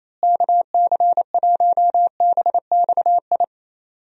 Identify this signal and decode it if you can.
Morse code, heard as a keyed tone, about 27 wpm, 700 Hz. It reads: KC1BXI
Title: USA Call Signs - 6 Characters - Repeated 2x - 28wpm